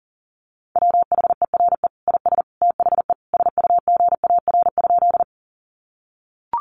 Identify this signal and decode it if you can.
Morse code, heard as a keyed tone, about 40 wpm, 700 Hz. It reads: WHERE IS THE SUGAR?
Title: Sentences from Top 1200 Words 40wpm